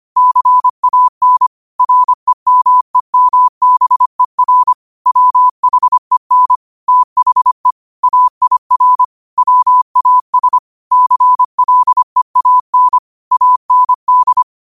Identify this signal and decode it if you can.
Morse code, heard as a keyed tone, about 25 wpm, 1000 Hz. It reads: CAN REMEMBER WHEN THE AIR WAS CLEAN AND